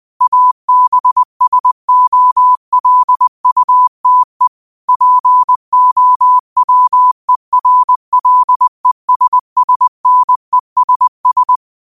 Morse code, heard as a keyed tone, about 20 wpm, 1000 Hz. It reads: ABSOLUTE POWERLESSNESS
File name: qotdfiles / 20wpm